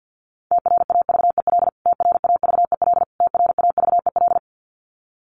Morse code, heard as a keyed tone, about 50 wpm, 700 Hz. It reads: TRAVEL TRAVEL TRAVEL
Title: Top 500 Words Repeated 50wpm